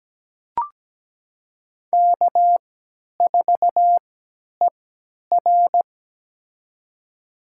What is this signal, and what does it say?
Morse code, heard as a keyed tone, about 17 wpm, spaced as for 10 wpm, 700 Hz. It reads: K4ER
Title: USA Call Signs - 4 Characters - 3x Character Spacing - 17wpm